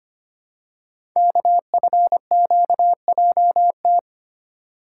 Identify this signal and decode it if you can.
Morse code, heard as a keyed tone, about 25 wpm, 700 Hz. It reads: KFQJT
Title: Code Groups - 5 Letters - 25wpm